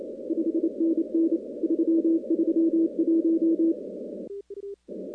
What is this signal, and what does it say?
Morse code, heard as a keyed tone, about 28 wpm, 330 Hz. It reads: 5NN 331
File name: contests / 2019-cw-wpx-kt5j-k5pi- -2330 0-15